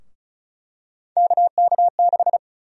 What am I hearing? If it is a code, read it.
Morse code, heard as a keyed tone, about 36 wpm, 700 Hz. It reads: KK6
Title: USA Call Sign Prefixes - Repeated 2x - 35wpm